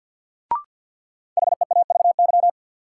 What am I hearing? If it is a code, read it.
Morse code, heard as a keyed tone, about 46 wpm, 700 Hz. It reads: HEAVY
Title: Sentences from Top 1100 Words - New Word Drill Alt - 50wpm